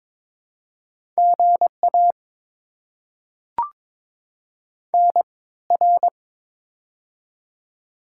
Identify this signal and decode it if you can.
Morse code, heard as a keyed tone, about 21 wpm, spaced as for 16 wpm, 690 Hz. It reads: GA NR
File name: Top 90 QSO Elements - 3x Character Spacing - 22wpm